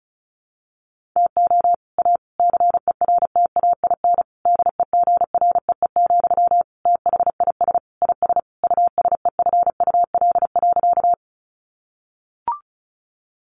Morse code, heard as a keyed tone, about 34 wpm, 700 Hz. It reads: TO A CERTAIN DEGREE, THIS IS USEFUL.
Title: Sentences from Top 1500 Words 35wpm